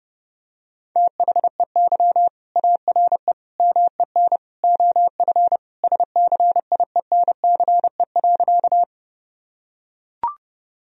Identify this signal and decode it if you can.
Morse code, heard as a keyed tone, about 30 wpm, 700 Hz. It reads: THEY ARE MEN OF SCIENCE.
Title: Sentences from Top 1000 Words - Review Part 2 - 30wpm